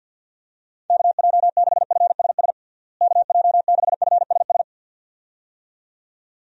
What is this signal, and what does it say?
Morse code, heard as a keyed tone, about 50 wpm, 700 Hz. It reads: KJ6FSS KJ6FSS
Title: USA Call Signs - 6 Characters - Repeated 2x - 50wpm